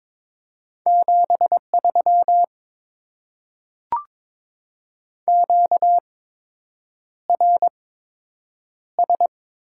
Morse code, heard as a keyed tone, about 22 wpm, spaced as for 8 wpm, 700 Hz. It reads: 73 QRS
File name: Top 90 QSO Elements - 8x Character Spacing - 22wpm